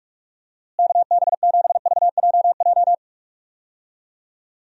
Morse code, heard as a keyed tone, about 46 wpm, 700 Hz. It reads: KB7VJJ
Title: USA Call Signs - 6 Characters - Repeated 3x - 45wpm